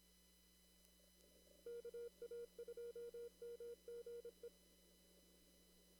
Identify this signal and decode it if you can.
Morse code, heard as a keyed tone, about 26 wpm, 470 Hz. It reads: KA2MGE
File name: CW / kt5j wpx cw-op-k5pi